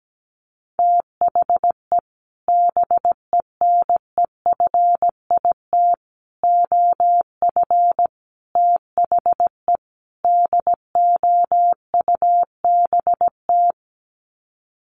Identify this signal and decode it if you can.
Morse code, heard as a keyed tone, about 17 wpm, 700 Hz. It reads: THE BENEFIT OF THE DOUBT
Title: Sets of 5 Words Encore 17wpm